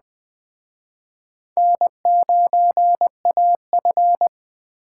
Morse code, heard as a keyed tone, about 20 wpm, 700 Hz. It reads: N9AF
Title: Speed Racing - USA 1x2 Call Signs 17wpm